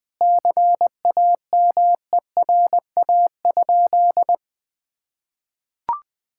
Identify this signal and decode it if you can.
Morse code, heard as a keyed tone, about 20 wpm, 700 Hz. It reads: CAMERA?
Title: Sentences from Top 1000 Words - Review Part 5 - 20wpm